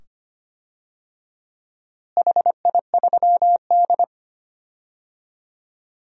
Morse code, heard as a keyed tone, about 25 wpm, 700 Hz. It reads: HI3D
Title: DXCC-100 Call Signs - 4 Characters - Repeated 2x - 25wpm